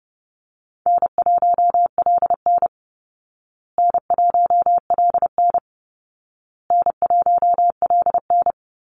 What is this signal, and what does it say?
Morse code, heard as a keyed tone, about 30 wpm, 700 Hz. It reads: N1LN N1LN N1LN
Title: Call Signs Repeated 2x - 30wpm